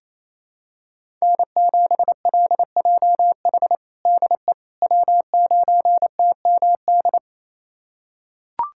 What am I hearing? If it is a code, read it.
Morse code, heard as a keyed tone, about 28 wpm, 700 Hz. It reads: N7LJH DE W9TMD
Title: USA Callsign DE Callsign - 5 Characters - 28wpm